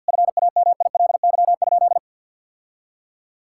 Morse code, heard as a keyed tone, about 50 wpm, 700 Hz. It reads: FAMILY?
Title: Sentences from Top 1000 Words - Review Part 6 - 50wpm